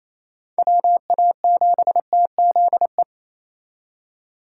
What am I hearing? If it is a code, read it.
Morse code, heard as a keyed tone, about 35 wpm, 700 Hz. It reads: WA7TZE
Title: Speed Racing - USA Call Signs 28wpm